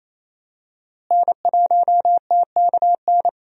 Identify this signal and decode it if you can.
Morse code, heard as a keyed tone, about 28 wpm, 700 Hz. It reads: N1TKN
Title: USA Call Signs - 5 Characters - Repeated 2x - 28wpm